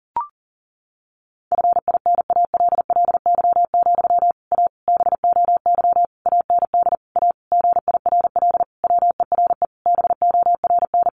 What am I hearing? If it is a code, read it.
Morse code, heard as a keyed tone, about 40 wpm, 700 Hz. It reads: FINALLY, A BOY AND A GIRL WERE BORN
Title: Sentences from Top 1200 Words 40wpm